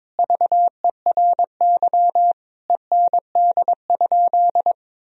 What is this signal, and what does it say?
Morse code, heard as a keyed tone, about 22 wpm, 700 Hz. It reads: VERY END?
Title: Sentences from Top 1000 Words - Review Part 3 - 22wpm